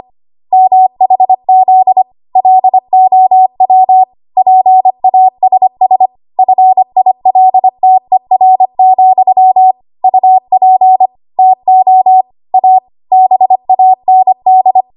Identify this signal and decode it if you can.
Morse code, heard as a keyed tone, about 25 wpm, 750 Hz. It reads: MHZ LOW PASS FILTER, UP TO A BAND
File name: Archive / 25 WPM / 25WPM